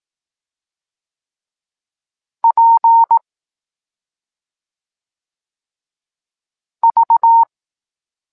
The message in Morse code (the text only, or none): P V